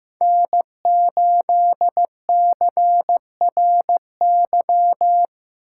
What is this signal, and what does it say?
Morse code, heard as a keyed tone, about 15 wpm, 700 Hz. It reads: N8CRY
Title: USA Call Signs - 5 Characters - 6x Character Spacing - 15wpm